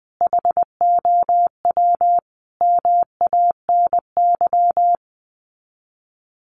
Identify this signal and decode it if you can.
Morse code, heard as a keyed tone, about 20 wpm, 700 Hz. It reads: HOW MANY
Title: Sets of 5 Words Encore 20wpm